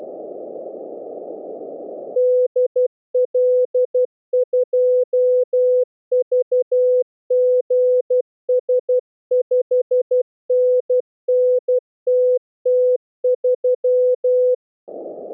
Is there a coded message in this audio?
Yes: Morse code from 2.2 to 14.5 s, heard as a keyed tone, about 12 wpm, 500 Hz. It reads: DL2VGS 5NN TT3